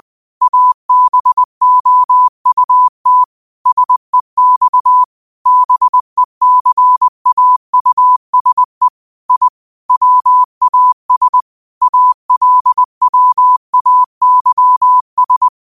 Morse code, heard as a keyed tone, about 20 wpm, 1000 Hz. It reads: ABOUT SEX BECAUSE I WAS ALWAYS